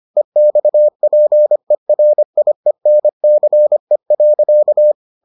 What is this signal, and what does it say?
Morse code, heard as a keyed tone, about 25 wpm, 600 Hz. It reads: EXPERIENCE.